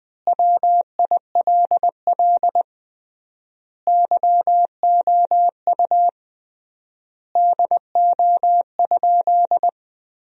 Morse code, heard as a keyed tone, about 20 wpm, 700 Hz. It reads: WILL YOU DO?
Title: 3x Word-Spacing - Sentences from Top 100 Words - 20wpm